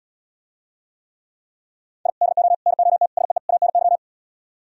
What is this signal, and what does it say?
Morse code, heard as a keyed tone, about 77 wpm, 700 Hz. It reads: I DO NOT SEE ANY